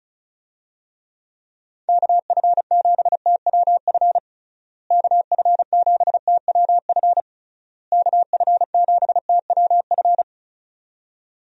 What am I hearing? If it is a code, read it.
Morse code, heard as a keyed tone, about 34 wpm, 700 Hz. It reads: KF7TWF KF7TWF KF7TWF
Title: USA Call Signs - 6 Characters - Repeated 3x - 35wpm